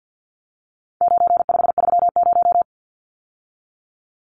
Morse code, heard as a keyed tone, about 50 wpm, 700 Hz. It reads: 9530
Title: Numbers - 4-Digit 50wpm